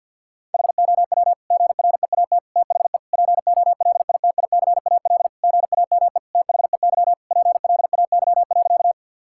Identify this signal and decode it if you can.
Morse code, heard as a keyed tone, about 50 wpm, 700 Hz. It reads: HOW GREAT THE POLITICAL GAME THEY PLAY.
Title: Sentences from Top 1000 Words - Review Part 5 - 50wpm